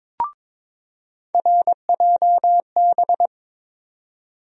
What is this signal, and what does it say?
Morse code, heard as a keyed tone, about 22 wpm, 700 Hz. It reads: RJB
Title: USA Call Sign Postfixes - 3 Letters 22wpm